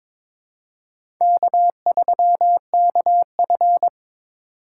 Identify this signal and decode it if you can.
Morse code, heard as a keyed tone, about 22 wpm, 700 Hz. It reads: K3KF
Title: USA Call Signs - 6x Character Spacing - 22wpm